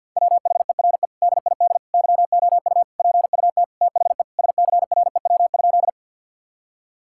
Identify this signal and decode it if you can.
Morse code, heard as a keyed tone, about 50 wpm, 700 Hz. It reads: WHERE DID YOU PUT THE SCREW?
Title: Sentences from Top 1600 Words 50wpm